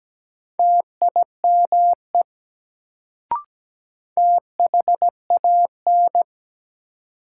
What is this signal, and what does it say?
Morse code, heard as a keyed tone, about 17 wpm, 700 Hz. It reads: TIME THAN
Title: Sentences from Top 100 Words - New Word Drill - 17wpm